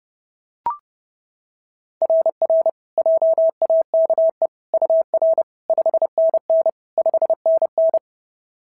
Morse code, heard as a keyed tone, about 31 wpm, 690 Hz. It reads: RR JAKE UR 5NN 5NN
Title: SOTA QSO Practice 30wpm